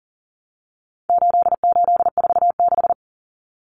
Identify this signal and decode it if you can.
Morse code, heard as a keyed tone, about 40 wpm, 700 Hz. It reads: 8846
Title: Numbers - 4-Digit 40wpm